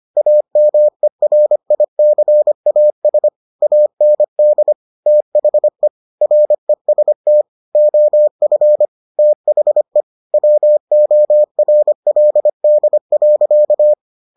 Morse code, heard as a keyed tone, about 25 wpm, 600 Hz. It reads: AMERICAS AND THE REST OF THE WORLD.